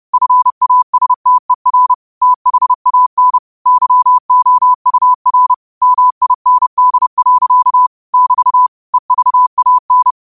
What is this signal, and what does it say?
Morse code, heard as a keyed tone, about 30 wpm, 1000 Hz. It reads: FASTER THAN YOUR MIND. <BT> EVAN
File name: qotdfiles / 30wpm